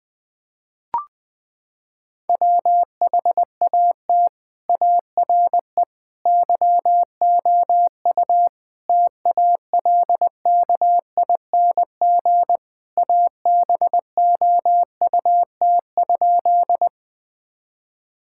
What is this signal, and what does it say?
Morse code, heard as a keyed tone, about 20 wpm, 700 Hz. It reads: WHAT ARE YOU TALKING ABOUT?
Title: Sentences with 5-Word Phrases - 20wpm